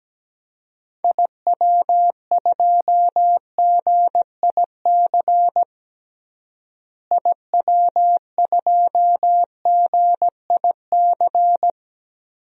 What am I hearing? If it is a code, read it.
Morse code, heard as a keyed tone, about 17 wpm, 700 Hz. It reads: IW2GIC IW2GIC
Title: DXCC-100 Call Signs - 6 Characters - Repeated 2x - 17wpm